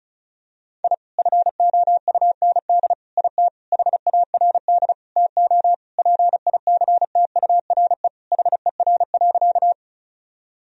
Morse code, heard as a keyed tone, about 34 wpm, 700 Hz. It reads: I FOUND IT HARD TO PICTURE HER.
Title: Sentences from Top 1000 Words - Review Part 1 - 35wpm